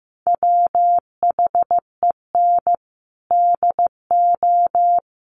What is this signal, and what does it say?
Morse code, heard as a keyed tone, about 15 wpm, 700 Hz. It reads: WHEN DO
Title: Sentences from Top 1100 Words 15wpm